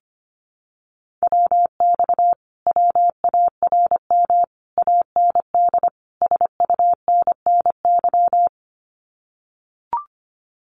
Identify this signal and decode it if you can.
Morse code, heard as a keyed tone, about 25 wpm, 700 Hz. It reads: WX WARM AND SUNNY
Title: Common QSO Phrases Encore 25wpm